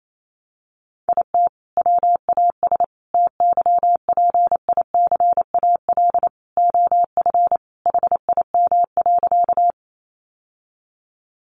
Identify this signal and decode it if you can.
Morse code, heard as a keyed tone, about 28 wpm, 700 Hz. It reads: IT WAS TYPICAL OF HIM.
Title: Sentences from Top 1100 Words 28wpm